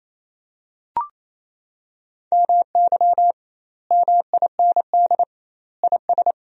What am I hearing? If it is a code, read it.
Morse code, heard as a keyed tone, about 28 wpm, 700 Hz. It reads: MY MIND IS
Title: 2x Word-Spacing - Sentences from Top 500 Words - 28wpm